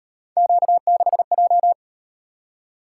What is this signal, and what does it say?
Morse code, heard as a keyed tone, about 38 wpm, 700 Hz. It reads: Q6J